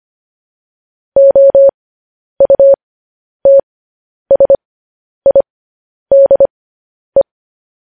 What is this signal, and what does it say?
Morse code, heard as a keyed tone, about 25 wpm, spaced as for 10 wpm, 550 Hz. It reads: OUTSIDE
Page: News Headlines in Morse Code at 10 WPM